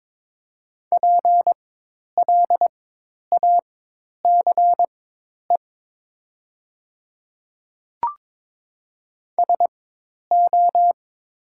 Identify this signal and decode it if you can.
Morse code, heard as a keyed tone, about 22 wpm, spaced as for 10 wpm, 700 Hz. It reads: PLACE SO
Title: 4x Character Spacing - Top 300 Words - 22wpm